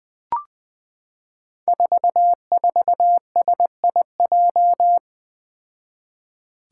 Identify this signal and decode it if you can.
Morse code, heard as a keyed tone, about 20 wpm, 700 Hz. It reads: 44SIJ
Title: Five Challenging Letter-Number Combinations 20wpm